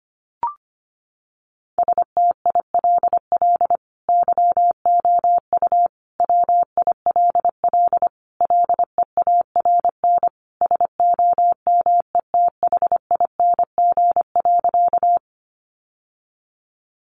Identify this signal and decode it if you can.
Morse code, heard as a keyed tone, about 25 wpm, 700 Hz. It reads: STILL YOU WILL LEARN SOMETHING.
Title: Sentences from Top 600 Words 25wpm